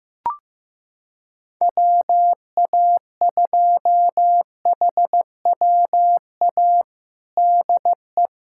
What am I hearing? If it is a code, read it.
Morse code, heard as a keyed tone, about 15 wpm, 700 Hz. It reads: WA2HWA DE
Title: USA Callsign DE Callsign - 6 Characters - 15wpm